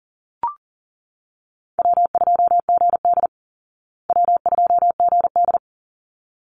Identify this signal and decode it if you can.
Morse code, heard as a keyed tone, about 40 wpm, 700 Hz. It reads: W2GD W2GD
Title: Easy Call Signs Repeated 2x - 40wpm